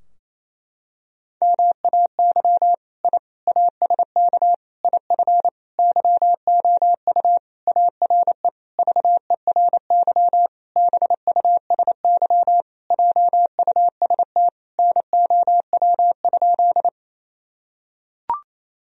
Morse code, heard as a keyed tone, about 28 wpm, 700 Hz. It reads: MAY I ASK IF YOU ARE VERY BUSY JUST NOW?
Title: Sentences from Top 1000 Words - Review Part 3 - 28wpm